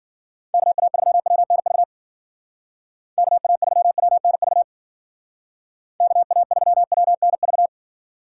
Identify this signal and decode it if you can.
Morse code, heard as a keyed tone, about 49 wpm, 700 Hz. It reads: KA3WNV KA3WNV KA3WNV
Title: Speed Racing - USA 2x3 Call Signs 40wpm